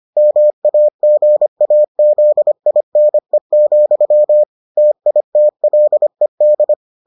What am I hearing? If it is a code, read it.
Morse code, heard as a keyed tone, about 25 wpm, 600 Hz. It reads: MAGAZINE, TITLED